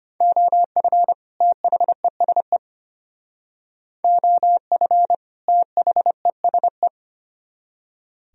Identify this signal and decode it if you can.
Morse code, heard as a keyed tone, about 33 wpm, 700 Hz. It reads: OF THESE OF THESE